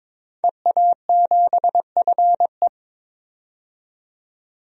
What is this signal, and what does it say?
Morse code, heard as a keyed tone, about 22 wpm, 700 Hz. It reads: EA7FE
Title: DXCC-100 Call Signs - 5 Characters - 22wpm